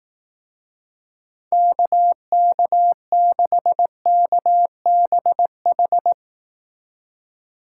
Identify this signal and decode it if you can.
Morse code, heard as a keyed tone, about 18 wpm, 700 Hz. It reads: KK6KBH
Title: Speed Racing - USA 2x3 Call Signs 15wpm